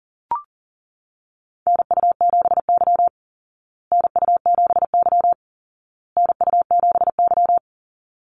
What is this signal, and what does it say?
Morse code, heard as a keyed tone, about 40 wpm, 700 Hz. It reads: NU7Y NU7Y NU7Y
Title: Call Signs Repeated 2x - 40wpm